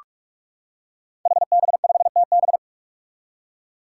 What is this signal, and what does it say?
Morse code, heard as a keyed tone, about 44 wpm, 700 Hz. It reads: HB5TB
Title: Five Challenging Letter-Number Combinations 45wpm